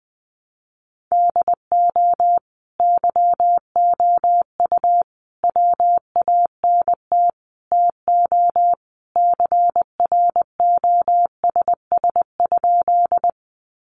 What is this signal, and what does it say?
Morse code, heard as a keyed tone, about 20 wpm, 700 Hz. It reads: DO YOU WANT TO CROSS?
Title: Sentences from Top 300 Words 20wpm